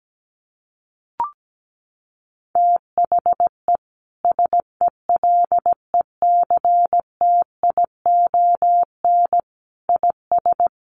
Morse code, heard as a keyed tone, about 17 wpm, 700 Hz. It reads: THE SELECTION IS
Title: Sentences from Top 1400 Words 17wpm